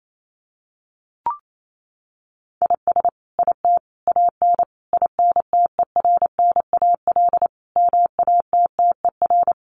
Morse code, heard as a keyed tone, about 28 wpm, 700 Hz. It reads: IS IT AN INTERNAL MATTER
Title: Sentences from Top 1400 Words 28wpm